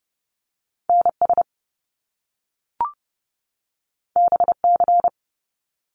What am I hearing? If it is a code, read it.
Morse code, heard as a keyed tone, about 30 wpm, 700 Hz. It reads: NS BC
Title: Canadian Provinces and Territories 30wpm